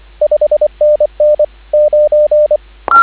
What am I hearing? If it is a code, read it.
Morse code, heard as a keyed tone, about 25 wpm, 590 Hz, with radio noise. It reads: 5NN 9
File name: CW / 2017 HA-OB HA5YG